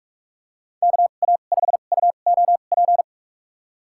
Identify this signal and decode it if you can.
Morse code, heard as a keyed tone, about 46 wpm, spaced as for 32 wpm, 700 Hz. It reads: KA5UOP
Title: USA Call Signs - 6 Characters - 2x Character Spacing - 45wpm